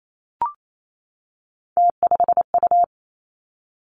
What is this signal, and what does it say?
Morse code, heard as a keyed tone, about 28 wpm, 700 Hz. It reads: T5U
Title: Three Challenging Letter-Number Combinations 28wpm